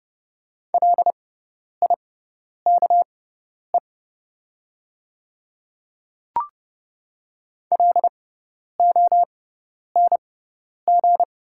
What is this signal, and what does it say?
Morse code, heard as a keyed tone, about 30 wpm, spaced as for 10 wpm, 700 Hz. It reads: LIKE LONG
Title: 6x Character Spacing - Top 200 Words - 30wpm